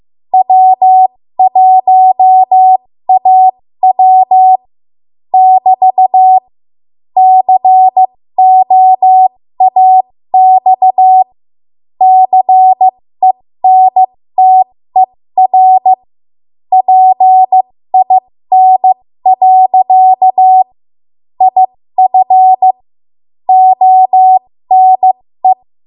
Morse code, heard as a keyed tone, about 15 wpm, spaced as for 13 wpm, 750 Hz. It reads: W1AW <BT> COAX CENTER PIN. IF ONE
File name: Morse / 13WPM